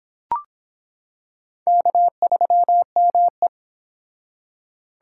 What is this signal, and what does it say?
Morse code, heard as a keyed tone, about 26 wpm, 700 Hz. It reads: K3ME